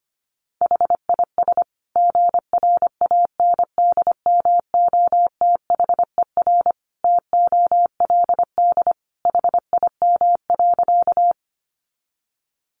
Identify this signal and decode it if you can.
Morse code, heard as a keyed tone, about 25 wpm, 700 Hz. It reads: HIS GRANDMOTHER TOLD HIM.
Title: Sentences from Top 1200 Words 25wpm